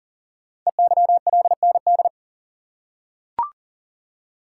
Morse code, heard as a keyed tone, about 43 wpm, 700 Hz. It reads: EYLND
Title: Code Groups - 5 Letters - 40wpm